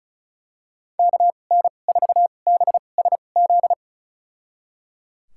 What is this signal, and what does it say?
Morse code, heard as a keyed tone, about 34 wpm, spaced as for 25 wpm, 700 Hz. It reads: KN4BSZ
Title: USA Call Signs - 6 Characters - 2x Character Spacing - 35wpm